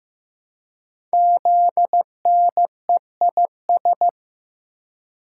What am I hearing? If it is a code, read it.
Morse code, heard as a keyed tone, about 15 wpm, 700 Hz. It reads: ZNEIS
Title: Code Groups - 5 Letters - 15wpm